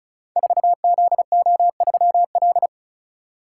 Morse code, heard as a keyed tone, about 36 wpm, 690 Hz. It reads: 4ZO3L
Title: Code Groups - 5 Letter:Numbers - 35wpm